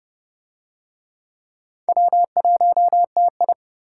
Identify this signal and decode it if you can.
Morse code, heard as a keyed tone, about 27 wpm, 700 Hz. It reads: W1TI